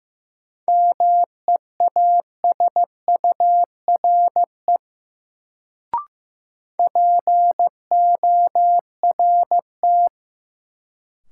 Morse code, heard as a keyed tone, about 15 wpm, 700 Hz. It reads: MEASURE PORT
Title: Sentences from Top 1000 Words - Review Part 1 - New Word Drill - 15wpm